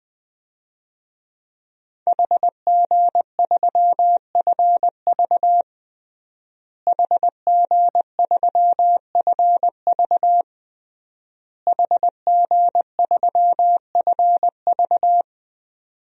Morse code, heard as a keyed tone, about 20 wpm, 700 Hz. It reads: HG3FV HG3FV HG3FV
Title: DXCC-100 Call Signs - 5 Characters - Repeated 3x - 20wpm